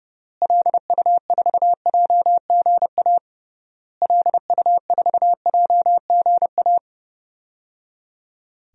Morse code, heard as a keyed tone, about 30 wpm, 700 Hz. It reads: LU4JGA LU4JGA
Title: DXCC-100 Call Signs - 6 Characters - Repeated 3x - 30wpm